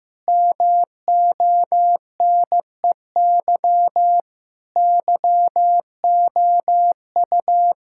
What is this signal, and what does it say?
Morse code, heard as a keyed tone, about 15 wpm, 700 Hz. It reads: MONEY YOU
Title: Sentences from Top 1000 Words - Review Part 6 - 15wpm